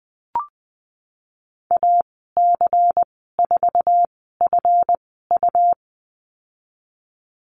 Morse code, heard as a keyed tone, about 20 wpm, spaced as for 14 wpm, 700 Hz. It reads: AC4FU
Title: 193 Lesson 31g - Letter v - Call Signs 20wpm